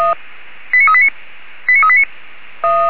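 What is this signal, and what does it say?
Five-tone selective call (ZVEI), heard as quick runs of five tones, tones of about 70 ms. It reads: 89389 89389
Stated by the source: ZVEI five-tone selective call 89406 received on an SDR.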